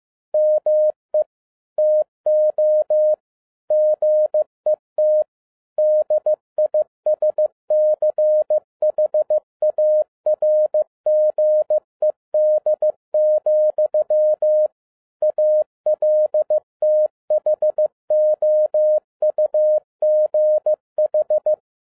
Morse code, heard as a keyed tone, about 15 wpm, 610 Hz. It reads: ME TO GET DISCHARGED, ALTHOUGH
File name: 2011-07-mp3 / n57 BACK FROM A BIG ACCIDENT 15wpm